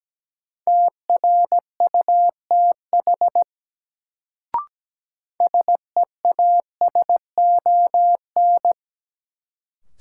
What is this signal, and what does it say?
Morse code, heard as a keyed tone, about 17 wpm, 700 Hz. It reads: TRUTH SEASON
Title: Sets of 5 Words - New Word Drill - 17wpm